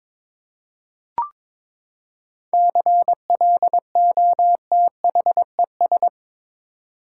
Morse code, heard as a keyed tone, about 21 wpm, 700 Hz. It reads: CLOTHES
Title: Seven Letter Words - 22wpm